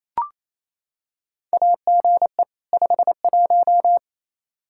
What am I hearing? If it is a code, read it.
Morse code, heard as a keyed tone, about 27 wpm, 700 Hz. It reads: AGE 51
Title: Practice QSO - Exchange 3 - More Info - 28wpm